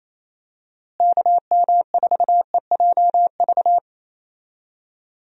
Morse code, heard as a keyed tone, about 28 wpm, 700 Hz. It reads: KM4EJV
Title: USA Call Signs - 6 Characters - Repeated 2x - 28wpm